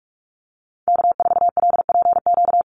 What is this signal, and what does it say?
Morse code, heard as a keyed tone, about 44 wpm, 700 Hz. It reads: K4LPQ
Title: NA SOTA Activator Callsigns 45wpm